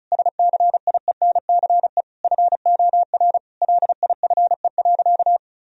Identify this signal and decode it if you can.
Morse code, heard as a keyed tone, about 34 wpm, 700 Hz. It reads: SCIENCE FOR LIFE.
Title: Sentences from Top 1000 Words - Review Part 2 - 35wpm